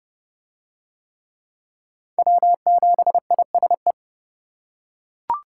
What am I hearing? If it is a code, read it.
Morse code, heard as a keyed tone, about 25 wpm, 700 Hz. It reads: W7ISE
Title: Speed Racing - USA 1x3 Call Signs 20wpm